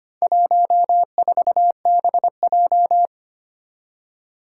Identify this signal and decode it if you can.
Morse code, heard as a keyed tone, about 25 wpm, 700 Hz. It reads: 14BJ
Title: Four Challenging Letter-Number Combinations 25wpm